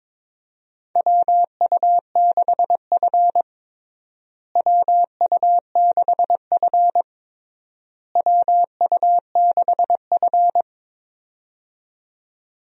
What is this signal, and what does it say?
Morse code, heard as a keyed tone, about 22 wpm, 700 Hz. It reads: WU6F WU6F WU6F
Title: USA Call Signs - 4 Characters - Repeated 3x - 22wpm